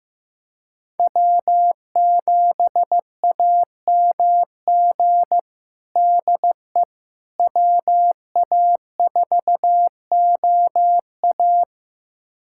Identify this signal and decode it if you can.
Morse code, heard as a keyed tone, about 15 wpm, 700 Hz. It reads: W7AMG DE WA4OA
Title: USA Callsign DE Callsign - 5 Characters - 15wpm